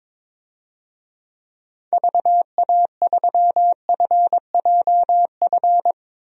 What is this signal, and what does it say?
Morse code, heard as a keyed tone, about 22 wpm, 700 Hz. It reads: VA3FJF
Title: DXCC-100 Call Signs - 6 Characters - 22wpm